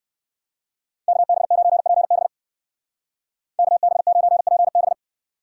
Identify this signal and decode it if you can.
Morse code, heard as a keyed tone, about 56 wpm, 700 Hz. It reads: KB9PB KB9PB